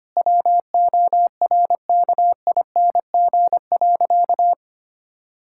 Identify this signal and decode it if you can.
Morse code, heard as a keyed tone, about 25 wpm, 700 Hz. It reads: WORKING.
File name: practice / 25wpm / Sentences with 3-Word Phrases - 25wpm